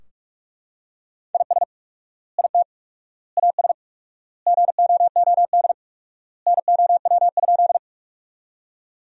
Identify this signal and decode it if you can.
Morse code, heard as a keyed tone, about 43 wpm, 700 Hz. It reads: IS IT AS GOOD NOW?
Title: 4x Word-Spacing - Sentences from Top 200 Words - 45wpm